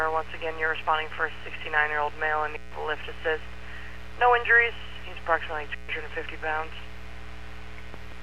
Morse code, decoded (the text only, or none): none